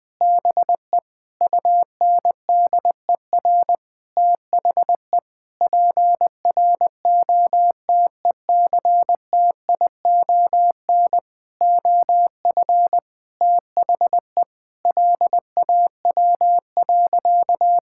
Morse code, heard as a keyed tone, about 19 wpm, 700 Hz. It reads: BE UNDER THE PROTECTION OF THE LAW.
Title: Sentences from Top 1600 Words 20wpm